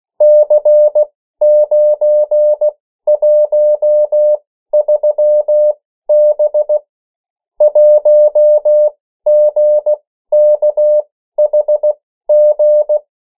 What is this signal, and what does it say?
Morse code, heard as a keyed tone, about 16 wpm, spaced as for 13 wpm, 600 Hz. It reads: C913B 1GKHG
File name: morse mp3 / 14 urok 13